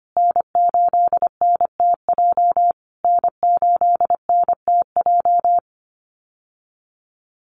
Morse code, heard as a keyed tone, about 25 wpm, 700 Hz. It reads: N8NTJ N8NTJ
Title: USA Call Signs Repeated 3x 25wpm